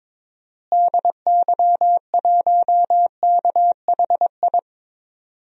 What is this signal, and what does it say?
Morse code, heard as a keyed tone, about 22 wpm, 700 Hz. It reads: DY1KHI
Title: DXCC-100 Call Signs - 6 Characters - Repeated 2x - 22wpm